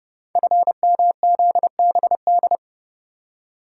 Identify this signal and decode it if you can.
Morse code, heard as a keyed tone, about 30 wpm, 700 Hz. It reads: FMZBD
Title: Code Groups - 5 Letters - 30wpm